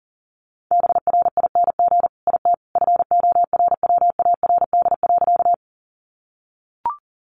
Morse code, heard as a keyed tone, about 40 wpm, 700 Hz. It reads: BRING IT FORWARD.
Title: Sentences from Top 1300 Words 40wpm